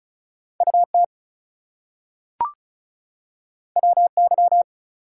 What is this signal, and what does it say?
Morse code, heard as a keyed tone, about 36 wpm, 700 Hz. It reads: UT WY
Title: US States and Canadian Provinces 35wpm